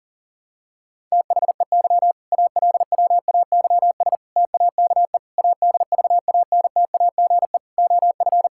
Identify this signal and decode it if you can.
Morse code, heard as a keyed tone, about 37 wpm, 700 Hz. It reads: THEY ALWAYS TAKE ADVANTAGE OF
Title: Sentences from Top 1000 Words - Review Part 6 - 40wpm